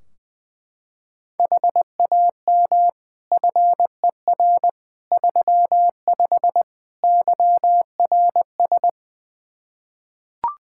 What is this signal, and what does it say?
Morse code, heard as a keyed tone, about 20 wpm, 700 Hz. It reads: HAM FER 35 YRS
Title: Practice QSO - Exchange 3 - More Info - 20wpm